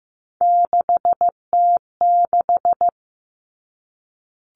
Morse code, heard as a keyed tone, about 15 wpm, 700 Hz. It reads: 6T6
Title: All Cut Numbers - 3-Digit 15wpm